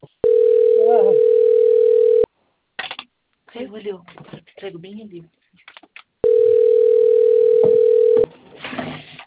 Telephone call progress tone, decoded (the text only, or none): ringback tone